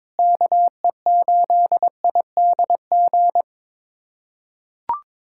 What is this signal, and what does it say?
Morse code, heard as a keyed tone, about 22 wpm, 700 Hz. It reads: KE8IDG
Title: USA Callsign DE Callsign 22wpm